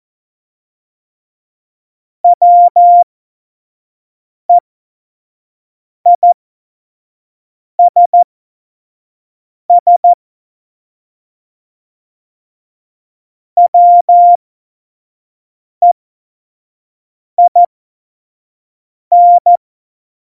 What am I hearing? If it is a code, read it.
Morse code, heard as a keyed tone, about 14 wpm, spaced as for 4 wpm, 700 Hz. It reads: WEISSWEIN
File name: Morsekurs / Witze / Witz 03.Dezember 2019, 25 F70